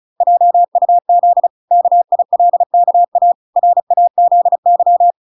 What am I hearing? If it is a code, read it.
Morse code, heard as a keyed tone, about 35 wpm, 700 Hz. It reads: JUZ KILKA RAZY